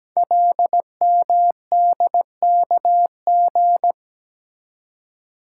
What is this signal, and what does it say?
Morse code, heard as a keyed tone, about 17 wpm, 700 Hz. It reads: LMDKG
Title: Code Groups - 5 Letters - 17wpm